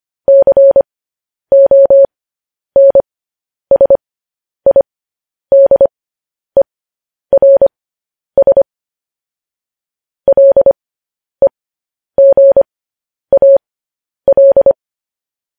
Morse code, heard as a keyed tone, about 25 wpm, spaced as for 10 wpm, 550 Hz. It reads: CONSIDERS LEGAL
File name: podcast / News headlines-1x-25x10